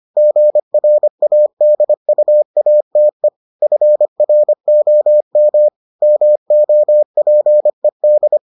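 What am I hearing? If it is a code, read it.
Morse code, heard as a keyed tone, about 25 wpm, 600 Hz. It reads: GRADUATE FROM MOPED
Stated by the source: Morse edition of the Foundations of Amateur Radio episodes.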